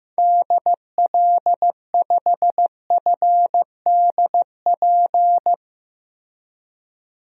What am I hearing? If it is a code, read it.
Morse code, heard as a keyed tone, about 15 wpm, 700 Hz. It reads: DL5FDP
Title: DXCC-100 Call Signs - 6 Characters - 15wpm